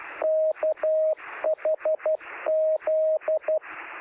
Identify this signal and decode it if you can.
Morse code, heard as a keyed tone, about 12 wpm, 630 Hz, with radio noise. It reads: KHZ
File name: sm7lcb / 2029z oh3shf 13cm